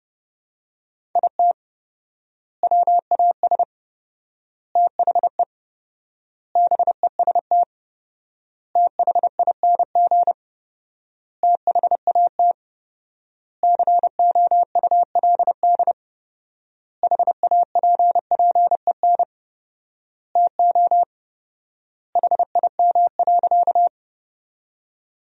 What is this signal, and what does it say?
Morse code, heard as a keyed tone, about 32 wpm, 700 Hz. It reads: IT WAS THE BEST THING THAT COULD HAPPEN TO HIM.
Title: 4x Word-Spacing - Sentences from Top 400 Words - 30wpm